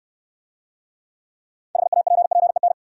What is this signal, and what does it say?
Morse code, heard as a keyed tone, about 69 wpm, 690 Hz. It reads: HAPPEN